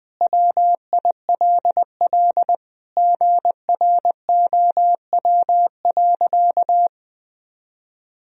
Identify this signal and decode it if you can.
Morse code, heard as a keyed tone, about 20 wpm, 700 Hz. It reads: WILL GROW.
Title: Sentences from Top 1000 Words - Review Part 1 - 20wpm